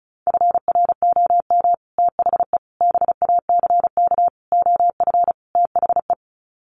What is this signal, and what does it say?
Morse code, heard as a keyed tone, about 34 wpm, 700 Hz. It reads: FROM THE BACK OF THE
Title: Sets of 5 Words Encore 35wpm